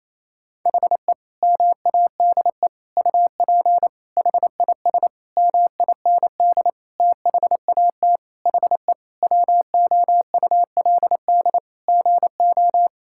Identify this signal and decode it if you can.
Morse code, heard as a keyed tone, about 28 wpm, 700 Hz. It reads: HE MADE UP HIS MIND THAT HE WOULD GO
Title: Sentences from Top 1600 Words 28wpm